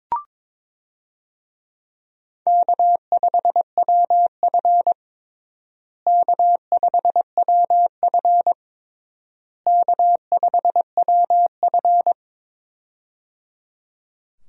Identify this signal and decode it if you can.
Morse code, heard as a keyed tone, about 22 wpm, 700 Hz. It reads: K5WF K5WF K5WF
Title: USA Call Signs - 4 Characters - Repeated 3x - 22wpm